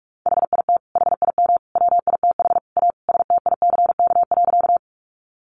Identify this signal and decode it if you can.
Morse code, heard as a keyed tone, about 44 wpm, 700 Hz. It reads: HIT HIM WITH A STICK.
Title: Sentences from Top 1200 Words 45wpm